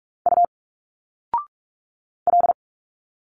Morse code, heard as a keyed tone, about 46 wpm, 700 Hz. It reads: U L
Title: Single Letters 45wpm